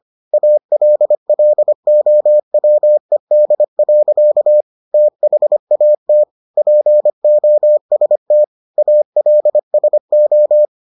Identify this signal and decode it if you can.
Morse code, heard as a keyed tone, about 25 wpm, 600 Hz. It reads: ALLOWED. THAT POST ALSO